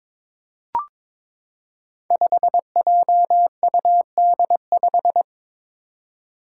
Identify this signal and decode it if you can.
Morse code, heard as a keyed tone, about 22 wpm, 700 Hz. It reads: 5JUD5
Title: Five Challenging Letter-Number Combinations 22wpm